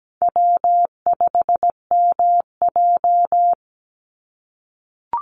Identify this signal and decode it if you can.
Morse code, heard as a keyed tone, about 17 wpm, 700 Hz. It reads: W5MJ
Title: Easy Call Signs - 17wpm